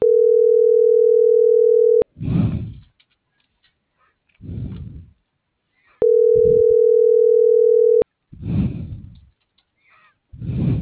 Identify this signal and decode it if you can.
Ringback tone (North America), 440 + 480 Hz.